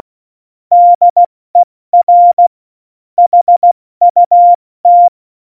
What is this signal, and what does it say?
Morse code, heard as a keyed tone, about 16 wpm, 700 Hz. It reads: DER HUT
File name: Morsekurs / Witze / Witz 01. Dezember 2020, 70 F80